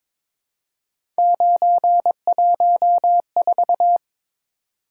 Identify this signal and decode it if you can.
Morse code, heard as a keyed tone, about 23 wpm, 700 Hz. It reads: 914
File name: practice / 22wpm / Three Letter-Number Combinations - 22wpm